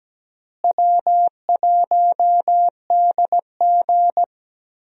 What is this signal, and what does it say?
Morse code, heard as a keyed tone, about 17 wpm, 700 Hz. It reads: W1DG
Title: USA Call Signs - 4 Characters - Repeated 3x - 17wpm